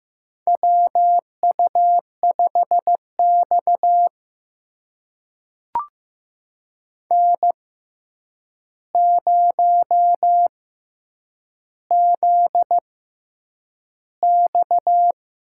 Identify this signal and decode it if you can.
Morse code, heard as a keyed tone, about 15 wpm, spaced as for 8 wpm, 700 Hz. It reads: WU5X N0ZX
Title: USA Call Signs - 4 Characters - 6x Character Spacing - 15wpm